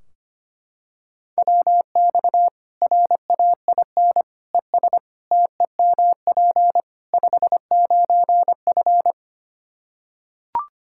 Morse code, heard as a keyed tone, about 25 wpm, 700 Hz. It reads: WX RAIN ES TEMP 59F
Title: Practice QSO - Exchange 2 - Info - 25wpm